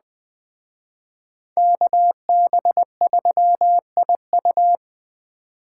Morse code, heard as a keyed tone, about 21 wpm, 700 Hz. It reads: KB3IU
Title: Speed Racing - USA Call Signs 17wpm